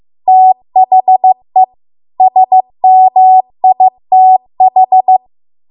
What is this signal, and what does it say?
Morse code, heard as a keyed tone, about 15 wpm, 750 Hz. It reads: THE SMITH